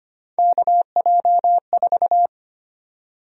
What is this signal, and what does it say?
Morse code, heard as a keyed tone, about 25 wpm, 700 Hz. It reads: KJ4
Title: USA Call Sign Prefixes - 25wpm